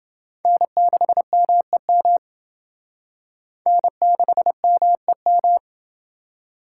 Morse code, heard as a keyed tone, about 28 wpm, 700 Hz. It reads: N6MEM N6MEM